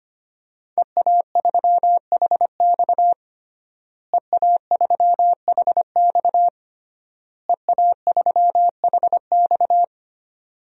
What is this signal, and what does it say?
Morse code, heard as a keyed tone, about 25 wpm, 700 Hz. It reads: EA3HX EA3HX EA3HX
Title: DXCC-100 Call Signs - 5 Characters - Repeated 3x - 25wpm